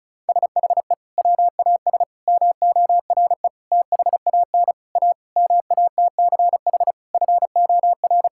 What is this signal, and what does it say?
Morse code, heard as a keyed tone, about 36 wpm, 700 Hz. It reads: SHE WAS MORE THAN A MATCH FOR
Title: Sentences from Top 1600 Words 35wpm